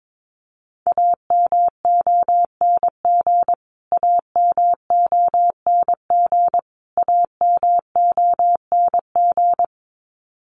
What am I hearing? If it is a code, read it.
Morse code, heard as a keyed tone, about 22 wpm, 700 Hz. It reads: AMONG AMONG AMONG
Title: Top 500 Words Repeated 22wpm